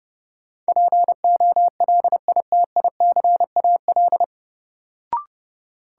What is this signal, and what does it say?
Morse code, heard as a keyed tone, about 31 wpm, 700 Hz. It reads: POLITICAL
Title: Sentences from Top 800 Words - New Word Drill - 30wpm